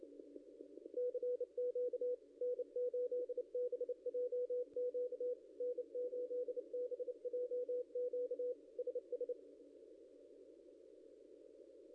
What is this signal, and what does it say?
Morse code, heard as a keyed tone, about 29 wpm, 490 Hz. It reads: CQ N8BJQ N8BJQ SS
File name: contests / ss-cw-k5tr-op-n5rz -0600